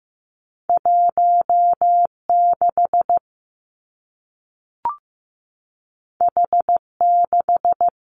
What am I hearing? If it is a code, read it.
Morse code, heard as a keyed tone, about 15 wpm, 700 Hz. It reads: 1 6 H 6
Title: Two Challenging Letter-Number Combinations 15wpm